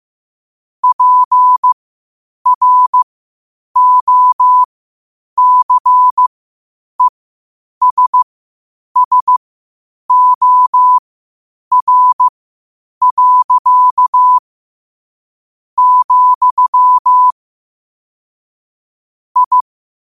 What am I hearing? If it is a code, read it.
Morse code, heard as a keyed tone, about 15 wpm, spaced as for 8 wpm, 1000 Hz. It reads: PROCESSOR. , I
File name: qotdfiles / 8wpm